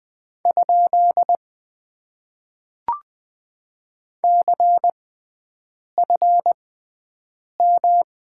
Morse code, heard as a keyed tone, about 20 wpm, spaced as for 7 wpm, 700 Hz. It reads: ? CFM
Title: Top 90 QSO Elements - 6x Character Spacing - 20wpm